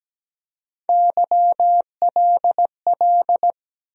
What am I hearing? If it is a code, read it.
Morse code, heard as a keyed tone, about 17 wpm, 700 Hz. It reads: YLL
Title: USA Call Sign Postfixes - 3 Letters 17wpm